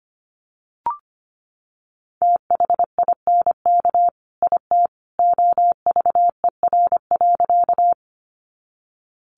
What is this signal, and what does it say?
Morse code, heard as a keyed tone, about 25 wpm, 700 Hz. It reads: THINK IT OVER.
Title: Sentences from Top 200 Words 25wpm